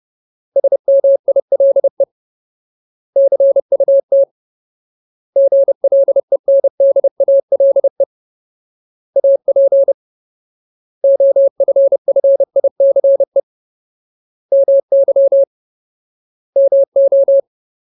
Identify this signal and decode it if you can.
Morse code, heard as a keyed tone, about 30 wpm, 550 Hz. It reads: SMILE CUT GLENDALE AP OFFICE MY MO
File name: Ultimate Word List / 30WPM Ultimate